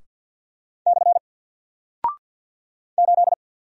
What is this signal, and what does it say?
Morse code, heard as a keyed tone, about 54 wpm, 700 Hz. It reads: / 8
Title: Single Character (Letter:Number:Punctuation:Prosign) Alt 50wpm